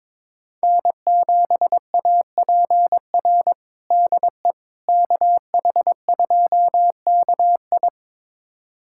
Morse code, heard as a keyed tone, about 22 wpm, 700 Hz. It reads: N7APR DE KH2KI
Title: USA Callsign DE Callsign - 5 Characters - 22wpm